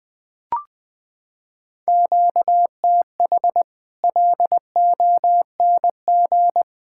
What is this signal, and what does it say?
Morse code, heard as a keyed tone, about 20 wpm, 700 Hz. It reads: QTH LONG
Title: Practice QSO - Exchange 1 - Beginning - 20wpm